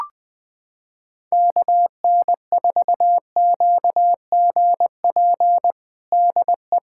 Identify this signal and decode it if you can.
Morse code, heard as a keyed tone, about 21 wpm, 700 Hz. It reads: KN4QGP DE
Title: USA Callsign DE Callsign 20wpm